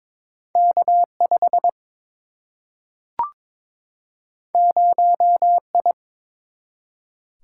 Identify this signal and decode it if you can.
Morse code, heard as a keyed tone, about 22 wpm, 700 Hz. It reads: K5 0I
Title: Single-Letter Single-Number 22wpm